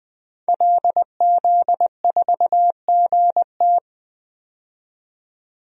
Morse code, heard as a keyed tone, about 20 wpm, 700 Hz. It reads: LZ4GT
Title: DXCC-100 Call Signs - 5 Characters - 20wpm